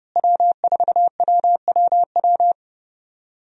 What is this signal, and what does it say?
Morse code, heard as a keyed tone, about 31 wpm, 700 Hz. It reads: W4WWW
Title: USA Call Signs - 5 Characters - Repeated 2x - 30wpm